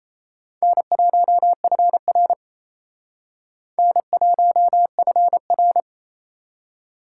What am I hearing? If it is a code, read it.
Morse code, heard as a keyed tone, about 34 wpm, 700 Hz. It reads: N1FR N1FR
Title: Speed Racing - USA 1x2 Call Signs 28wpm